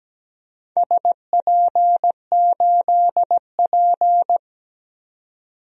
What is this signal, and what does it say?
Morse code, heard as a keyed tone, about 17 wpm, 700 Hz. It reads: SP8P
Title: DXCC-100 Call Signs - 4 Characters - 17wpm